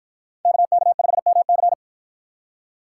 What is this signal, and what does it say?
Morse code, heard as a keyed tone, about 50 wpm, 700 Hz. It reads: KK5MC